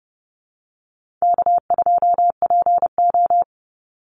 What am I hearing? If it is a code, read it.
Morse code, heard as a keyed tone, about 30 wpm, 700 Hz. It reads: K2PO
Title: Call Signs - 30wpm